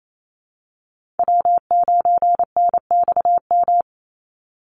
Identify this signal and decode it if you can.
Morse code, heard as a keyed tone, about 28 wpm, 700 Hz. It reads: W9NXM
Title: Call Signs Repeated 2x - 28wpm